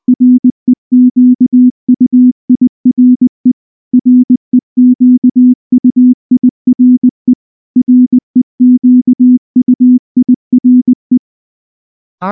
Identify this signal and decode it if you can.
Morse code, heard as a keyed tone, about 20 wpm, 250 Hz. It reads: REQUIRE REQUIRE REQUIRE
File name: morse / morse-training-20WPM-9